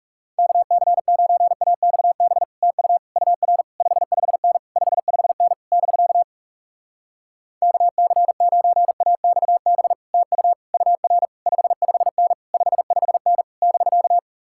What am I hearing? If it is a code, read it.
Morse code, heard as a keyed tone, about 38 wpm, 700 Hz. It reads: KC9AXB TU UR 55N 55N <BK> KC9AXB TU UR 55N 55N <BK>
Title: Speed-Racing Common QSO Phrases Encore - 30wpm